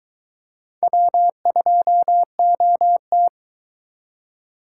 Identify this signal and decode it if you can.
Morse code, heard as a keyed tone, about 22 wpm, 700 Hz. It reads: W2OT